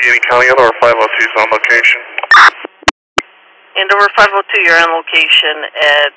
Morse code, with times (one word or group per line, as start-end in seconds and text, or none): none